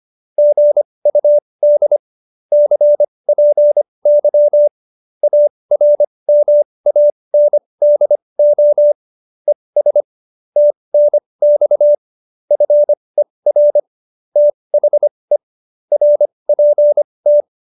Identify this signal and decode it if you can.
Morse code, heard as a keyed tone, about 25 wpm, spaced as for 20 wpm, 590 Hz. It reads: GUD CPY ARMANDO ES TNX FER THE RPT
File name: qso215b 20